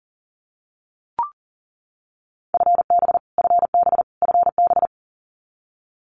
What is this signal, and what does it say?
Morse code, heard as a keyed tone, about 40 wpm, 700 Hz. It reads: FB FB FB
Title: Abbreviations Repeated 3x - 40wpm